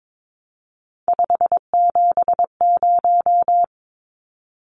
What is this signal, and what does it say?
Morse code, heard as a keyed tone, about 22 wpm, 700 Hz. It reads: 570
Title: Numbers - 3-Digit 22wpm